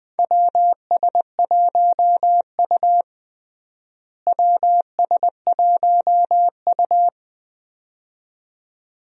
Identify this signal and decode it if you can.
Morse code, heard as a keyed tone, about 21 wpm, 700 Hz. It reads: WS1U WS1U
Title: USA Call Signs - 4 Characters - Repeated 2x - 20wpm